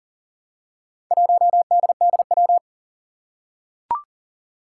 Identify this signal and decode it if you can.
Morse code, heard as a keyed tone, about 40 wpm, 700 Hz. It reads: 1DDW
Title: Four Challenging Letter-Number Combinations 40wpm